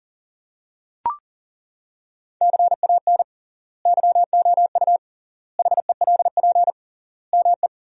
Morse code, heard as a keyed tone, about 37 wpm, 700 Hz. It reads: CAN YOU HELP ME
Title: 3x Word-Spacing - Sentences from Top 200 Words - 40wpm